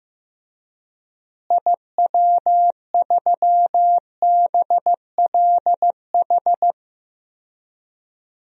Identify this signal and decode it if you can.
Morse code, heard as a keyed tone, about 15 wpm, 700 Hz. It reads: IW3BLH
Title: DXCC-100 Call Signs - 6 Characters - Repeated 3x - 15wpm